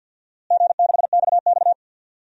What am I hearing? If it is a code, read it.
Morse code, heard as a keyed tone, about 49 wpm, 700 Hz. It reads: G6XX
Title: DXCC-100 Call Signs - 4 Characters - Repeated 2x - 50wpm